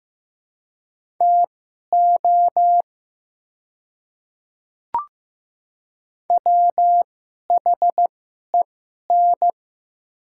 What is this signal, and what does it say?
Morse code, heard as a keyed tone, about 15 wpm, spaced as for 11 wpm, 690 Hz. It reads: TO WHEN
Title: 2x Character Spacing - Top 100 Words - 15wpm